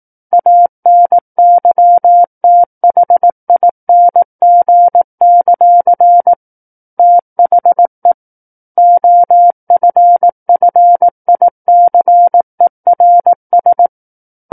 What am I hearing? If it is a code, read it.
Morse code, heard as a keyed tone, about 18 wpm, 700 Hz. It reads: ANYTHING; THE OFFICERS
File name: 18 WPM